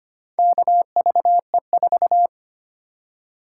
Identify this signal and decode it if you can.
Morse code, heard as a keyed tone, about 25 wpm, 700 Hz. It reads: KVE4
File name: practice / 25wpm / Four Letter-Number Combinations - 25wpm